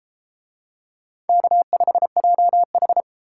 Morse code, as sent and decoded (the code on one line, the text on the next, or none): -.- ..... .--- ....
K5JH